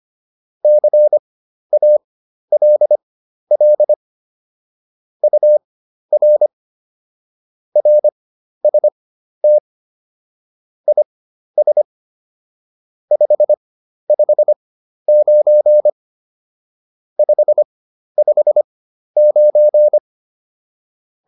Morse code, heard as a keyed tone, about 25 wpm, spaced as for 12 wpm, 600 Hz. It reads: CALL UR RST IS 559 559